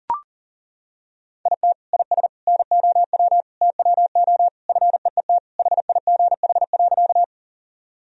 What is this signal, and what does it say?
Morse code, heard as a keyed tone, about 40 wpm, 700 Hz. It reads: IT IS NOW TWO FEET HIGH.
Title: Sentences from Top 1000 Words - Review Part 1 - 40wpm